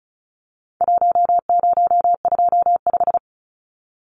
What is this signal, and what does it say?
Morse code, heard as a keyed tone, about 34 wpm, 700 Hz. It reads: 1025
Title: Numbers - Contest Style - 4-Digit 35wpm